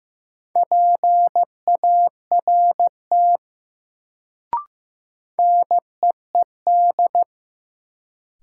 Morse code, heard as a keyed tone, about 15 wpm, 700 Hz. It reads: PART NEED
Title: Sentences from Top 200 Words - New Word Drill - 15wpm